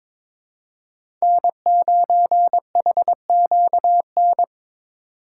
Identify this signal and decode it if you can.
Morse code, heard as a keyed tone, about 22 wpm, 700 Hz. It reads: N9HQN
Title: USA Call Signs - 2x Character Spacing - 22wpm